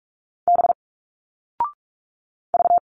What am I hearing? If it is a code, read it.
Morse code, heard as a keyed tone, about 46 wpm, 700 Hz. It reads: B V
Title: Single Letters 45wpm